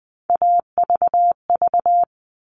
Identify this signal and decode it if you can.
Morse code, heard as a keyed tone, about 20 wpm, 700 Hz. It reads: AVV
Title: All Cut Numbers - 3-Digit 20wpm